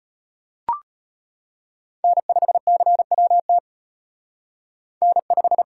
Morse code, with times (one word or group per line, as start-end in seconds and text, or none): 2.0-3.6 N5CWT
5.0-5.6 N5